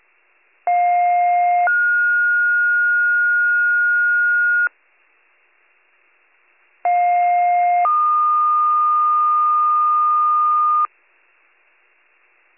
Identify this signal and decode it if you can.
Two-tone page, heard as a one-second tone, then a three-second tone, about 710 then 1430 Hz, about 710 then 1220 Hz. It is heard from 0.7 to 10.9 s.